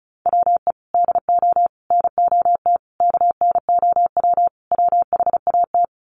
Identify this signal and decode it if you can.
Morse code, heard as a keyed tone, about 34 wpm, 700 Hz. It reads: WE DO NOT KNOW WHAT
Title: Sets of 5 Words 35wpm